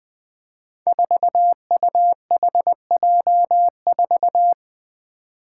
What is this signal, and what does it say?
Morse code, heard as a keyed tone, about 20 wpm, 700 Hz. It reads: 4UHJ4
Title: Five Challenging Letter-Number Combinations 20wpm